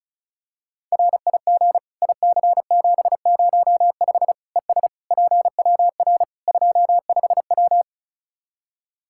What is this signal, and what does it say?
Morse code, heard as a keyed tone, about 36 wpm, 700 Hz. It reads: RIG IC705 ES PWR 25W
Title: Practice QSO - Exchange 2 - Info - 35wpm